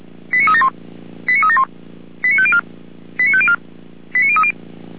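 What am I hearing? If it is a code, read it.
Five-tone selective call (ZVEI), heard as quick runs of five tones, tones of about 70 ms. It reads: 89371 89371 89504 89504 89030
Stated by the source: ZVEI five-tone call 89033, received by SDR, from feuerwehr-orlamuende (fire brigade Orlamünde).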